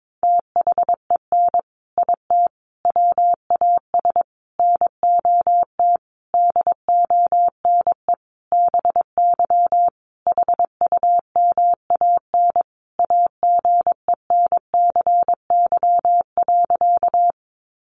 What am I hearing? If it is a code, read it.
Morse code, heard as a keyed tone, about 22 wpm, 700 Hz. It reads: THEN IT WAS NOT DONE BY HUMAN AGENCY.
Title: Sentences from Top 1300 Words 22wpm